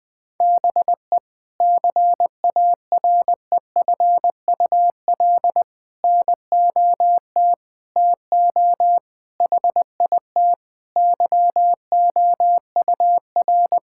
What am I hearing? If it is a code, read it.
Morse code, heard as a keyed tone, about 21 wpm, 700 Hz. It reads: BE CAREFUL NOT TO HIT YOUR
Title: Sentences from Top 1600 Words 20wpm